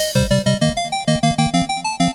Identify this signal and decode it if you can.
There is no code.